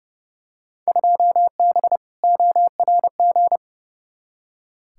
Morse code, heard as a keyed tone, about 30 wpm, 700 Hz. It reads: 2B ORG
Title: Field Day Exchanges Encore - 30wpm